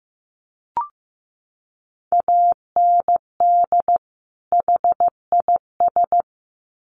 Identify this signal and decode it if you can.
Morse code, heard as a keyed tone, about 15 wpm, 700 Hz. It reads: AND HIS
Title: Sets of 2 Words 15wpm